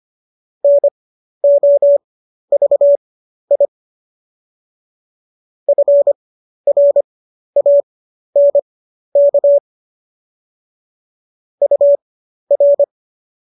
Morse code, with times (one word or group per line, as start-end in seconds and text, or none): 0.6-3.6 NOVI
5.7-9.6 FRANK
11.6-12.8 UR